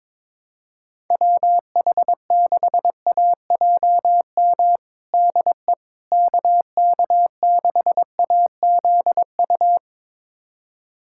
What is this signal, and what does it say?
Morse code, heard as a keyed tone, about 22 wpm, 700 Hz. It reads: WH6AJM DE KK6AZU
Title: USA Callsign DE Callsign 22wpm